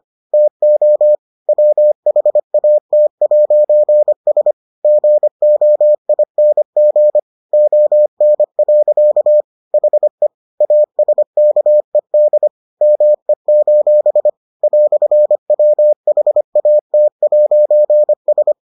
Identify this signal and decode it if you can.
Morse code, heard as a keyed tone, about 25 wpm, 600 Hz. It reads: TO WHAT'S GOING ON. HE ASKED ME: "WHAT'S